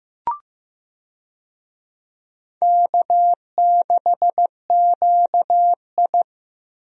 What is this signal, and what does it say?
Morse code, heard as a keyed tone, about 15 wpm, 700 Hz. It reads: K6QI
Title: USA Call Signs - 4 Characters - Repeated 3x - 15wpm